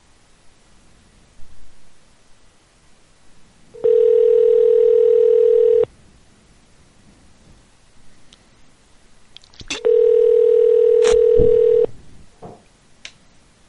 Ringback tone (North America), 440 + 480 Hz.